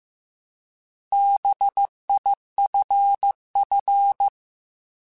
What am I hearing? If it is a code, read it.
Morse code, heard as a keyed tone, about 15 wpm, 800 Hz. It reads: BIFF
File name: Alphabetical / Alphabetical mp3s / Bs @13